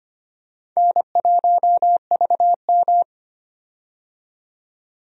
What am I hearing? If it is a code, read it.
Morse code, heard as a keyed tone, about 25 wpm, 700 Hz. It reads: N1VM
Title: USA Call Signs - 4 Characters - 25wpm